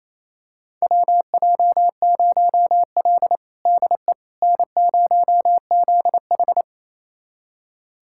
Morse code, heard as a keyed tone, about 28 wpm, 700 Hz. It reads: WJ0L DE N0ZH
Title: USA Callsign DE Callsign - 4 Characters - 28wpm